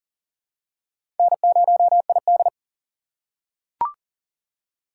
Morse code, heard as a keyed tone, about 40 wpm, 700 Hz. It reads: N0ID
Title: USA Call Signs - 4 Characters - 3x Character Spacing - 40wpm